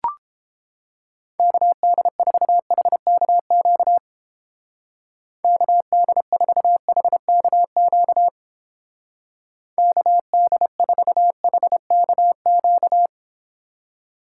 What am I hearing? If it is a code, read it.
Morse code, heard as a keyed tone, about 29 wpm, 700 Hz. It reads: KD4HKQ KD4HKQ KD4HKQ